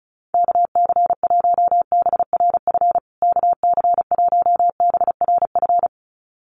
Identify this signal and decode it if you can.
Morse code, heard as a keyed tone, about 34 wpm, 700 Hz. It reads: KC1BRF KC1BRF
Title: USA Call Signs Repeated 2x 35wpm